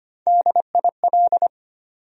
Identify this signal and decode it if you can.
Morse code, heard as a keyed tone, about 25 wpm, 700 Hz. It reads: DIL
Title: USA Call Sign Postfixes - 3 Letters 25wpm